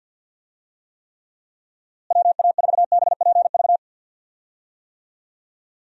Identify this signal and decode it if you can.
Morse code, heard as a keyed tone, about 51 wpm, 700 Hz. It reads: WA4BPV
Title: USA Call Signs - 6 Characters - Repeated 3x - 50wpm